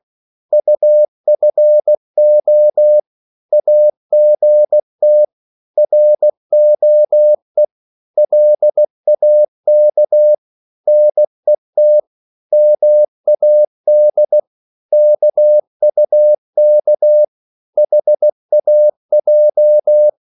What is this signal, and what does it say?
Morse code, heard as a keyed tone, about 16 wpm, 600 Hz. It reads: UFO AGT ROE LAK NET MAD KUK HAJ
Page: Korte danske ord 16wpm